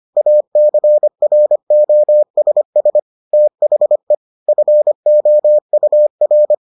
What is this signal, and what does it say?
Morse code, heard as a keyed tone, about 25 wpm, 600 Hz. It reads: ACROSS THE FOUR